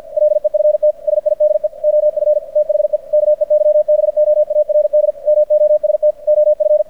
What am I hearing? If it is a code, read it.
Morse code, heard as a keyed tone, about 50 wpm, 590 Hz. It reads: LEFT SIDE OF THE KEYBOARD MOST OF
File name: archived / CRSnet